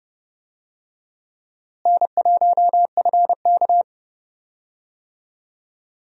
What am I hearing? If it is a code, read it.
Morse code, heard as a keyed tone, about 30 wpm, 700 Hz. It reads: N1FK
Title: USA Call Signs - 4 Characters - Repeated 3x - 30wpm